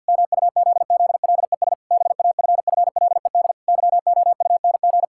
Morse code, heard as a keyed tone, about 46 wpm, 700 Hz. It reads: MUZZLES BAFFLED YOUNG
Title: Pangram Challenge v2 - 50wpm